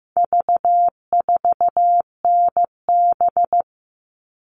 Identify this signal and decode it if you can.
Morse code, heard as a keyed tone, about 15 wpm, 700 Hz. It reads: V4NB
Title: All Cut Numbers - 4-Digit 15wpm